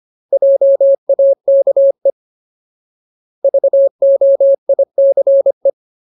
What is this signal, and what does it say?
Morse code, heard as a keyed tone, about 25 wpm, 550 Hz. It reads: JAKE VOICE
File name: Ultimate Word List / 25WPM Ultimate